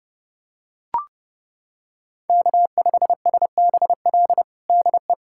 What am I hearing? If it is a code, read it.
Morse code, heard as a keyed tone, about 30 wpm, 700 Hz. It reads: K5SBL DE
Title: USA Callsign DE Callsign - 5 Characters - 30wpm